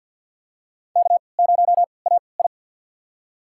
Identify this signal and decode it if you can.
Morse code, heard as a keyed tone, about 47 wpm, spaced as for 28 wpm, 700 Hz. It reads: K0AI
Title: USA Call Signs - 3x Character Spacing - 50wpm